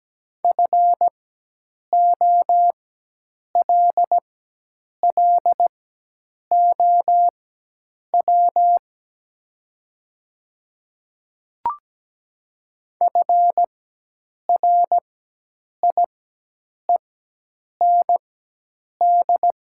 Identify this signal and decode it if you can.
Morse code, heard as a keyed tone, about 17 wpm, spaced as for 8 wpm, 700 Hz. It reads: FOLLOW FRIEND
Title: 4x Character Spacing - Top 300 Words - 17wpm